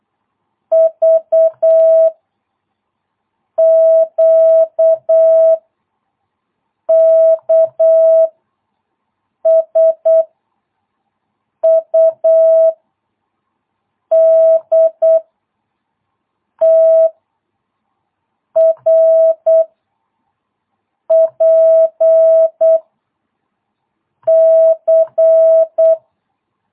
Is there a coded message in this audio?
Yes: Morse code from 0.7 to 25.9 s, heard as a keyed tone, about 8 wpm, 650 Hz. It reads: V Q K S U D T R P C